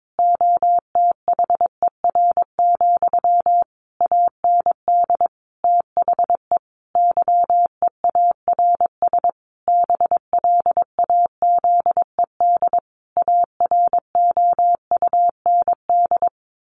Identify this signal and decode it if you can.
Morse code, heard as a keyed tone, about 22 wpm, 700 Hz. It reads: OTHER, AND THE YEARS BLAZED AROUND
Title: A Sound Of Thunder 22wpm